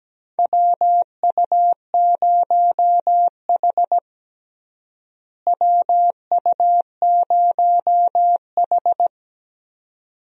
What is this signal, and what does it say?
Morse code, heard as a keyed tone, about 17 wpm, 700 Hz. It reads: WU0H WU0H
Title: USA Call Signs - 4 Characters - Repeated 2x - 17wpm